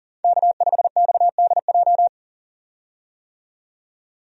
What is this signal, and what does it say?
Morse code, heard as a keyed tone, about 37 wpm, 700 Hz. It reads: K5XDJ
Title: USA Call Signs - 5 Characters - 40wpm